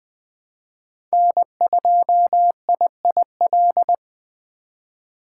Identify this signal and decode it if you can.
Morse code, heard as a keyed tone, about 20 wpm, 700 Hz. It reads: N2IIL
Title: USA Call Signs - 3x Character Spacing - 20wpm